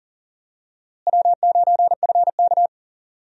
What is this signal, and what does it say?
Morse code, heard as a keyed tone, about 40 wpm, 700 Hz. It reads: W9FK
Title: USA Call Signs - 4 Characters - Repeated 2x - 40wpm